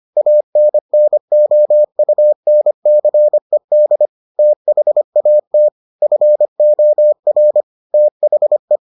Morse code, heard as a keyed tone, about 25 wpm, 600 Hz. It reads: ANNOUNCED THAT FOR THE